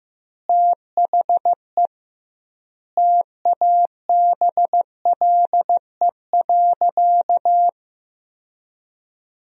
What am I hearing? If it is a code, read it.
Morse code, heard as a keyed tone, about 16 wpm, 700 Hz. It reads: THE TABLE.
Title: 2x Word-Spacing - Sentences from Top 500 Words - 15wpm